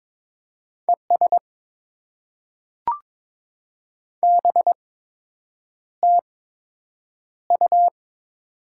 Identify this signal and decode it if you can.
Morse code, heard as a keyed tone, about 22 wpm, spaced as for 8 wpm, 700 Hz. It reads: ES BTU
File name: Top 20 QSO Elements - 8x Character Spacing - 22wpm